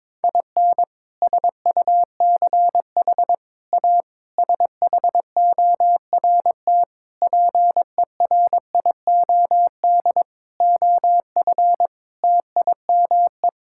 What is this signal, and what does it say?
Morse code, heard as a keyed tone, about 22 wpm, 700 Hz. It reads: IN SUCH A SHORT PERIOD OF TIME
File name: practice / 22wpm / Sets of 7 Words - 22wpm